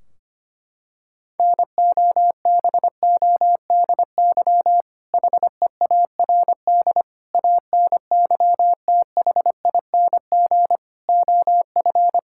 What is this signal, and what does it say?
Morse code, heard as a keyed tone, about 25 wpm, 700 Hz. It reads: NOBODY HEARD ANYTHING OF
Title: Sentences from Top 1600 Words 25wpm